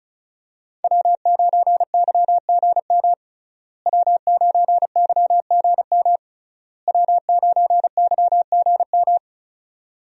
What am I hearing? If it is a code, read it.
Morse code, heard as a keyed tone, about 36 wpm, 700 Hz. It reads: W9YGM W9YGM W9YGM
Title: USA Call Signs - 5 Characters - Repeated 3x - 35wpm